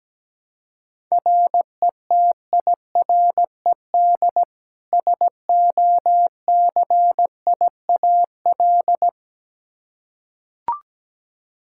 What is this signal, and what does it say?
Morse code, heard as a keyed tone, about 17 wpm, 700 Hz. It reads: RETIRED SOCIAL
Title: Practice QSO - Exchange 3 - More Info - 17wpm